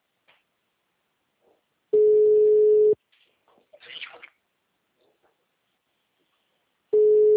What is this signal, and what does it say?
Ringback tone (Europe, 425 Hz).